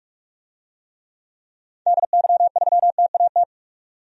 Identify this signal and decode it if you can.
Morse code, heard as a keyed tone, about 42 wpm, 700 Hz. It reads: DY3TAT
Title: DXCC-100 Call Signs - 6 Characters - 45wpm